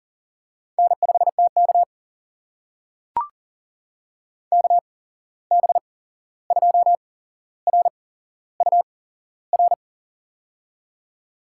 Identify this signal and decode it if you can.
Morse code, heard as a keyed tone, about 40 wpm, spaced as for 15 wpm, 700 Hz. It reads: N5TK KB2RUR
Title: USA Call Signs - 8x Character Spacing - 40wpm